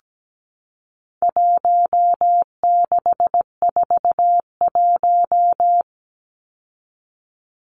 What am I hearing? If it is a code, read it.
Morse code, heard as a keyed tone, about 17 wpm, 700 Hz. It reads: 1641
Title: Numbers - Contest Style - 4-Digit 17wpm